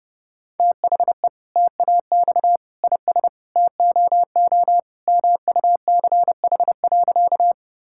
Morse code, heard as a keyed tone, about 30 wpm, 700 Hz. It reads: THE TAX IS TOO MUCH.
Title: Sentences from Top 1000 Words - Review Part 5 - 30wpm